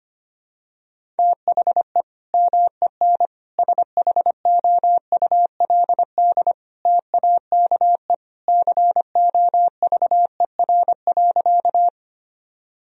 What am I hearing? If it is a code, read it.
Morse code, heard as a keyed tone, about 25 wpm, 700 Hz. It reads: THE MEN SHOULD TAKE COVER.
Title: Sentences from Top 1000 Words - Review Part 1 - 25wpm